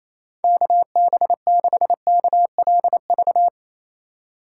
Morse code, heard as a keyed tone, about 37 wpm, 700 Hz. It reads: KB6KLV